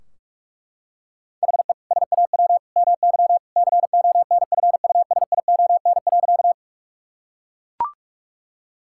Morse code, heard as a keyed tone, about 41 wpm, 700 Hz. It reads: HE SAW MY CONFUSION.
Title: Sentences from Top 1600 Words 45wpm